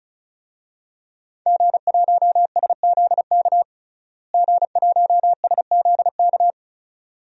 Morse code, heard as a keyed tone, about 36 wpm, 700 Hz. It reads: G1SZK G1SZK
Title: DXCC-100 Call Signs - 5 Characters - Repeated 2x - 35wpm